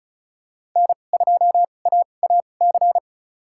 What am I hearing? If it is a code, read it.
Morse code, heard as a keyed tone, about 36 wpm, spaced as for 25 wpm, 700 Hz. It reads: N2AAC
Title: USA Call Signs - 5 Characters - 2x Character Spacing - 35wpm